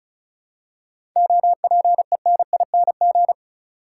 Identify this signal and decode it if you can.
Morse code, heard as a keyed tone, about 34 wpm, 700 Hz. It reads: OPENING
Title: Seven Letter Words - 35wpm